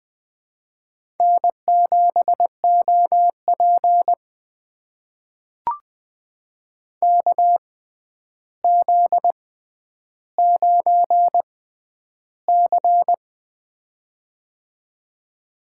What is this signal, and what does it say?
Morse code, heard as a keyed tone, about 20 wpm, spaced as for 10 wpm, 700 Hz. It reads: N7OP KZ9C
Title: USA Call Signs - 4 Characters - 6x Character Spacing - 20wpm